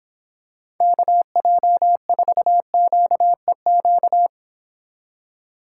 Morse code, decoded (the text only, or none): KJ4QEQ